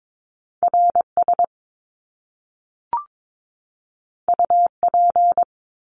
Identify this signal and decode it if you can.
Morse code, heard as a keyed tone, about 22 wpm, 700 Hz. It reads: RS UP
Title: Top Two Letter Combinations in Words 22wpm